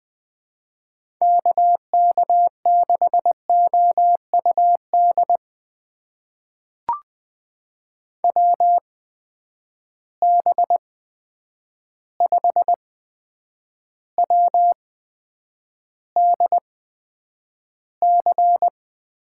Morse code, heard as a keyed tone, about 20 wpm, spaced as for 9 wpm, 700 Hz. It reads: KK6OUD WB5WDC
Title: USA Call Signs - 6 Characters - 8x Character Spacing - 20wpm